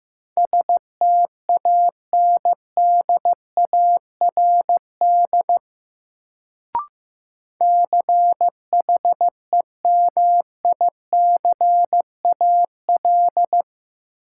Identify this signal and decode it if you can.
Morse code, heard as a keyed tone, about 15 wpm, 700 Hz. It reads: STANDARD CHEMICAL
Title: Sentences from Top 1200 Words - New Word Drill Alt - 15wpm